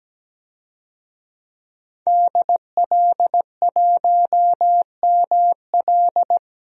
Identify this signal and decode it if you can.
Morse code, heard as a keyed tone, about 17 wpm, 700 Hz. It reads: DL1ML
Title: DXCC-100 Call Signs - 5 Characters - 17wpm